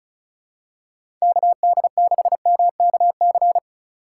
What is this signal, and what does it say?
Morse code, heard as a keyed tone, about 34 wpm, 700 Hz. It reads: KD6MKC
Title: USA Call Signs - 6 Characters - 6x Character Spacing - 35wpm